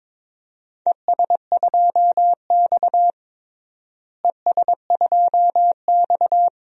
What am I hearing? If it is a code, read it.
Morse code, heard as a keyed tone, about 22 wpm, 700 Hz. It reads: ES2X ES2X
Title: DXCC-100 Call Signs - 4 Characters - Repeated 2x - 22wpm